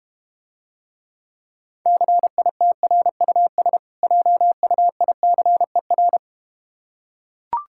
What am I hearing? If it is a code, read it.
Morse code, heard as a keyed tone, about 32 wpm, 700 Hz. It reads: CITRUS JUICER